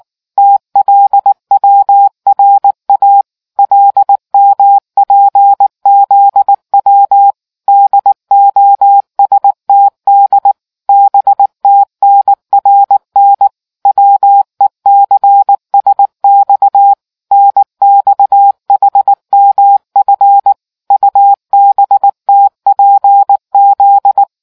Morse code, heard as a keyed tone, about 19 wpm, 800 Hz. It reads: TLWRA LMPZW DOSTD BTNRN WECSX NXHMF UBTPZ